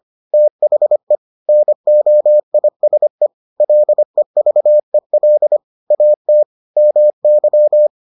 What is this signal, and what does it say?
Morse code, heard as a keyed tone, about 25 wpm, 600 Hz. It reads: THE NOISE LEVEL AT MY